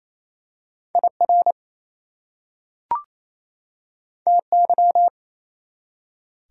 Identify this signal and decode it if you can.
Morse code, heard as a keyed tone, about 28 wpm, 700 Hz. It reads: IR TY
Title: Prefixes and Suffixes 28wpm